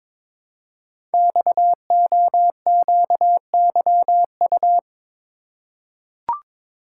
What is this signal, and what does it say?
Morse code, heard as a keyed tone, about 22 wpm, 700 Hz. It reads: XOQYU
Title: Code Groups - 5 Letters - 22wpm